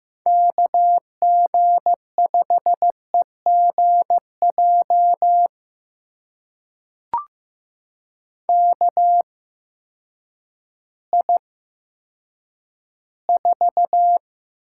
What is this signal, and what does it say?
Morse code, heard as a keyed tone, about 15 wpm, spaced as for 9 wpm, 700 Hz. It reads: KG5EGJ KI4
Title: USA Call Signs - 8x Character Spacing - 15wpm